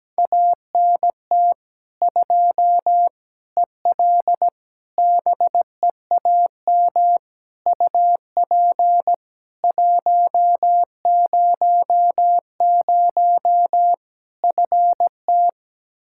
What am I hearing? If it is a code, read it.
Morse code, heard as a keyed tone, about 17 wpm, 700 Hz. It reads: ANT 2 EL BEAM UP 100 FT
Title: Practice QSO - Exchange 2 - Info - 17wpm